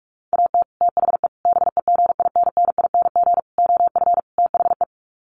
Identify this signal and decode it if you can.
Morse code, heard as a keyed tone, about 44 wpm, 700 Hz. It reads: AT THE BEGINNING OF THE
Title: Sets of 5 Words 45wpm